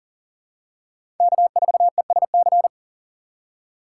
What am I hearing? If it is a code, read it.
Morse code, heard as a keyed tone, about 37 wpm, 700 Hz. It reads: K4ESC
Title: USA Call Signs - 5 Characters - 2x Character Spacing - 40wpm